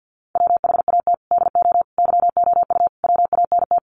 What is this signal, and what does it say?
Morse code, heard as a keyed tone, about 48 wpm, 700 Hz. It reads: WHAT DO YOU WANT
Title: Sets of 4 Words 50wpm